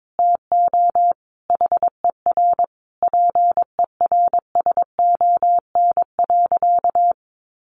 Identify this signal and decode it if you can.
Morse code, heard as a keyed tone, about 22 wpm, 700 Hz. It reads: TO HER PERSON.
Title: Sentences from Top 1400 Words 22wpm